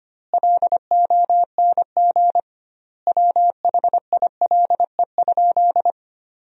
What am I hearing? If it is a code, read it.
Morse code, heard as a keyed tone, about 25 wpm, 700 Hz. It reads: LONG WHILE?
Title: 2x Word-Spacing - Sentences from Top 300 Words - 25wpm